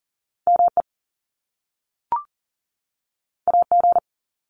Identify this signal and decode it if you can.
Morse code, heard as a keyed tone, about 40 wpm, 700 Hz. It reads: ME AG
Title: Top Two Letter Combinations in Words 40wpm